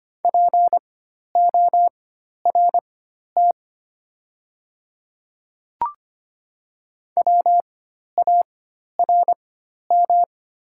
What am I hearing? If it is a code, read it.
Morse code, heard as a keyed tone, about 25 wpm, spaced as for 12 wpm, 700 Hz. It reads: PORT WARM
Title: 4x Character Spacing - Top 500 Words - 25wpm